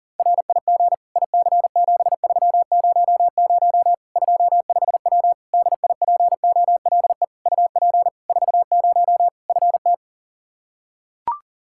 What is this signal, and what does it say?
Morse code, heard as a keyed tone, about 40 wpm, 700 Hz. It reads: RIG IC7300 25W DIPOLE UP 40 FT
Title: K1USN SST QSO Practice 40wpm